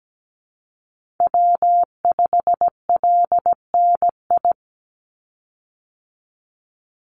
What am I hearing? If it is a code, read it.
Morse code, heard as a keyed tone, about 17 wpm, 700 Hz. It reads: W5LNI
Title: Call Signs - 17wpm